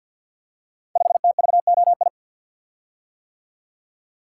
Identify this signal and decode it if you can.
Morse code, heard as a keyed tone, about 53 wpm, 700 Hz. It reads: 5TVOI
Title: Code Groups - 5 Letter:Numbers - 50wpm